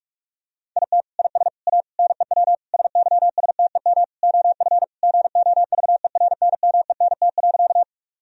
Morse code, heard as a keyed tone, about 42 wpm, 700 Hz. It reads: IT IS A NEW SYSTEM OF GOVERNMENT.
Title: Sentences from Top 1000 Words - Review Part 3 - 45wpm